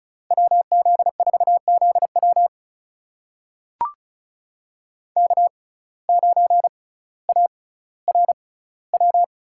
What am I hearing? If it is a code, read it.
Morse code, heard as a keyed tone, about 36 wpm, spaced as for 18 wpm, 700 Hz. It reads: WZ4ZW K9ARW
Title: USA Call Signs - 5 Characters - 6x Character Spacing - 35wpm